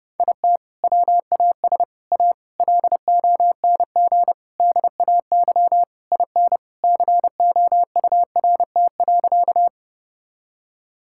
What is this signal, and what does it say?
Morse code, heard as a keyed tone, about 32 wpm, 700 Hz. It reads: IT WAS A LONG DAY IN COURT.
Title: Sentences from Top 1600 Words 30wpm